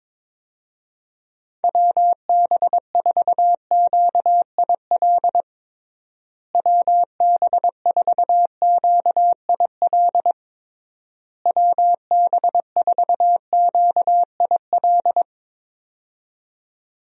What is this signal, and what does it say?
Morse code, heard as a keyed tone, about 22 wpm, 700 Hz. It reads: WB4QIL WB4QIL WB4QIL
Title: USA Call Signs - 6 Characters - Repeated 3x - 22wpm